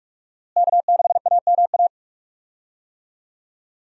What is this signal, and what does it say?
Morse code, heard as a keyed tone, about 43 wpm, 700 Hz. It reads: K6AMA
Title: USA Call Signs - 5 Characters - Repeated 2x - 45wpm